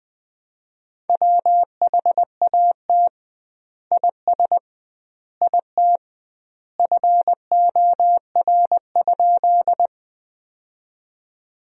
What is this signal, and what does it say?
Morse code, heard as a keyed tone, about 21 wpm, 700 Hz. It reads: WHAT IS IT FOR?
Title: 2x Word-Spacing - Sentences from Top 100 Words - 20wpm